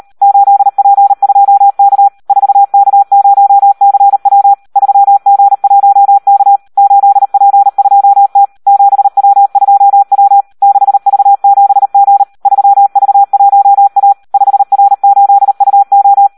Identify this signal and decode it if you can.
Morse code, heard as a keyed tone, about 38 wpm, 800 Hz. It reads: 8P2K 4K0CW 3G1K 8P2T 7W2W 6V7G 3V1A 5R8UO
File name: Audio practica CW / SD 1 38